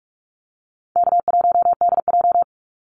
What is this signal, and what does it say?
Morse code, heard as a keyed tone, about 44 wpm, 700 Hz. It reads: K1DJ
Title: Easy Call Signs Repeated 3x - 45wpm